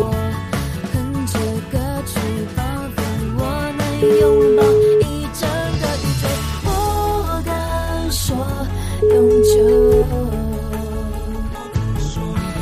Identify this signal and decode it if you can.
Ringback tone (Europe, 425 Hz), with noise.